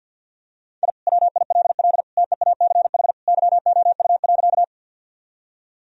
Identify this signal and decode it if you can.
Morse code, heard as a keyed tone, about 50 wpm, 700 Hz. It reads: I WILL TEACH YOU.
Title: Sentences from Top 1000 Words - Review Part 2 - 50wpm